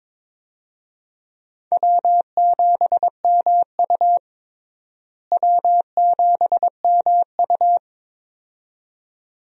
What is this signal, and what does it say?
Morse code, heard as a keyed tone, about 22 wpm, 700 Hz. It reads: W7MU W7MU
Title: USA Call Signs - 4 Characters - Repeated 2x - 22wpm